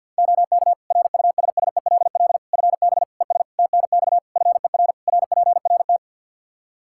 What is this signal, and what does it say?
Morse code, heard as a keyed tone, about 52 wpm, 700 Hz. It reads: OK RUSSELL FB ES TNX FER RPRT
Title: Practice QSO - Exchange 2 - Info - 50wpm